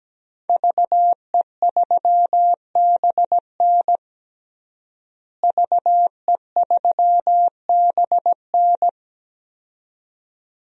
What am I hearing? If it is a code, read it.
Morse code, heard as a keyed tone, about 17 wpm, 700 Hz. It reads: VE3BN VE3BN
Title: DXCC-100 Call Signs - 5 Characters - Repeated 2x - 17wpm